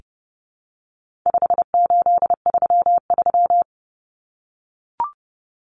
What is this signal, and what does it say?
Morse code, heard as a keyed tone, about 29 wpm, 700 Hz. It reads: 5833
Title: Numbers - 4-Digit 30wpm